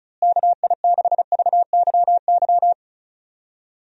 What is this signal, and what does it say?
Morse code, heard as a keyed tone, about 36 wpm, 700 Hz. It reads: KI6VYY
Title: USA Call Signs - 6 Characters - Repeated 3x - 35wpm